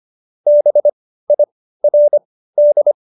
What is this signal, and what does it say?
Morse code, heard as a keyed tone, about 25 wpm, spaced as for 15 wpm, 590 Hz. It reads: BIRD
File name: Jokes-25-char-15-eff